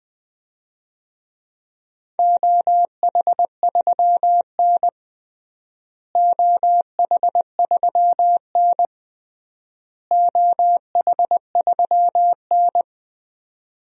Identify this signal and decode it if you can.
Morse code, heard as a keyed tone, about 21 wpm, 700 Hz. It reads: OH3N OH3N OH3N
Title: DXCC-100 Call Signs - 4 Characters - Repeated 3x - 20wpm